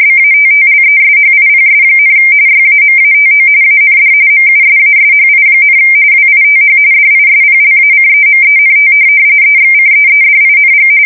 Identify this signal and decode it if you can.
RTTY, about 45.45 baud. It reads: BATTLESHIPS, YAMATO AND MUSASHI, WITH DISPLACEMENT EXCEEDING
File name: fsk170 / USS-New-Jersey